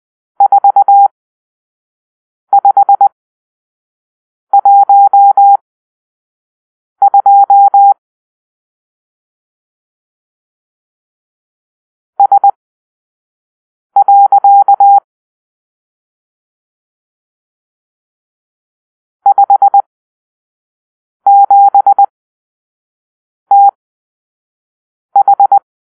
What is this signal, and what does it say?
Morse code, heard as a keyed tone, about 20 wpm, spaced as for 5 wpm, 800 Hz. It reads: 4512 S. 57TH
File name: arrl / Grp-6 Drill-4